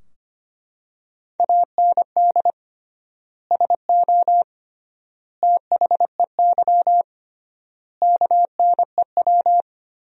Morse code, heard as a keyed tone, about 25 wpm, 700 Hz. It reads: AND SO THEY KNEW
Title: 3x Word-Spacing - Sentences from Top 400 Words - 25wpm